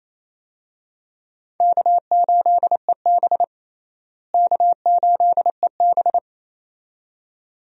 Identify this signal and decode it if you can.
Morse code, heard as a keyed tone, about 28 wpm, 700 Hz. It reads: K8EB K8EB
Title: USA Call Signs - 4 Characters - Repeated 2x - 28wpm